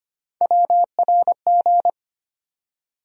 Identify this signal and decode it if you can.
Morse code, heard as a keyed tone, about 25 wpm, 700 Hz. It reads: WRG
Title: USA Call Sign Postfixes - 3 Letters 25wpm